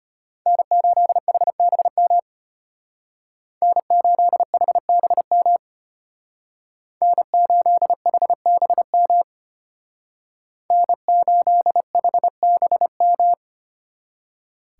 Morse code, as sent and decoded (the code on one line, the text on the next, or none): -. ---.. .... -... -- / -. ---.. .... -... -- / -. ---.. .... -... -- / -. ---.. .... -... --
N8HBM N8HBM N8HBM N8HBM